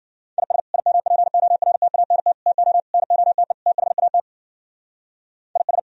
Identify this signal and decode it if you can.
Morse code, heard as a keyed tone, about 52 wpm, 700 Hz. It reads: IS IMPORTANT TO NOTE THAT IS